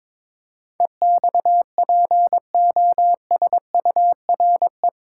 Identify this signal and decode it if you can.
Morse code, heard as a keyed tone, about 22 wpm, 700 Hz. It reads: EXPOSURE
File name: practice / 22wpm / Eight Letter Words - 22wpm